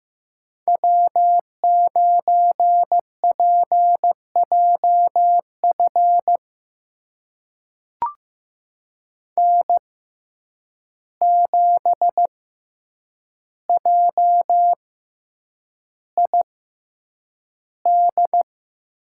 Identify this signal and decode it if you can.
Morse code, heard as a keyed tone, about 15 wpm, spaced as for 8 wpm, 700 Hz. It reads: W9PJF N7JID
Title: USA Call Signs - 5 Characters - 6x Character Spacing - 15wpm